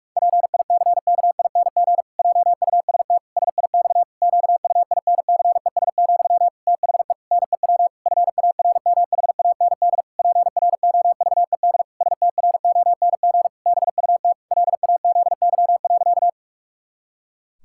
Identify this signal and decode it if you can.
Morse code, heard as a keyed tone, about 46 wpm, 700 Hz. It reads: PICKING JUST SIX QUINCES, THE NEW FARMHAND PROVED STRONG BUT LAZY.
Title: Pangram Challenge v2 - 45wpm